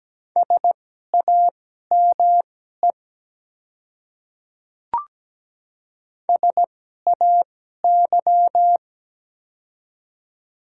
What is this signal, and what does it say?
Morse code, heard as a keyed tone, about 17 wpm, spaced as for 12 wpm, 700 Hz. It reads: SAME SAY
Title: 2x Character Spacing - Top 200 Words - 17wpm